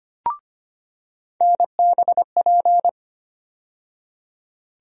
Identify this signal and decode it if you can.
Morse code, heard as a keyed tone, about 25 wpm, 700 Hz. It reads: NBP
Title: USA Call Sign Postfixes - 3 Letters 25wpm